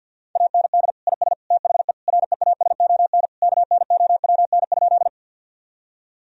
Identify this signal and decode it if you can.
Morse code, heard as a keyed tone, about 50 wpm, 700 Hz. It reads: AND IS THE REASON KNOWN?
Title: Sentences from Top 1000 Words - Review Part 3 - 50wpm